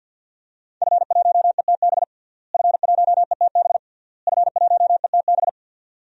Morse code, heard as a keyed tone, about 52 wpm, 700 Hz. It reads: F1ETB F1ETB F1ETB
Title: DXCC-100 Call Signs - 5 Characters - Repeated 3x - 50wpm